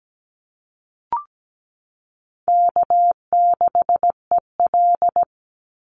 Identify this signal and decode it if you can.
Morse code, heard as a keyed tone, about 17 wpm, 700 Hz. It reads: K6EL
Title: NA SOTA Chaser Callsigns 17wpm